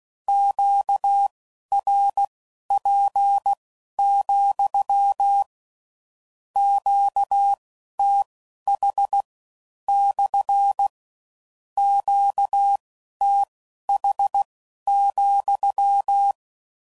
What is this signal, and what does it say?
Morse code, heard as a keyed tone, about 16 wpm, spaced as for 11 wpm, 800 Hz. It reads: QRP, QTH / QTH,
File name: CW Code Groups @15WPM / Q Calls / Q Calls SPSL2